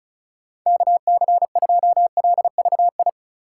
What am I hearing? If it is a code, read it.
Morse code, heard as a keyed tone, about 34 wpm, 700 Hz. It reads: KC2LVI
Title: USA Call Signs - 6 Characters - 4x Character Spacing - 35wpm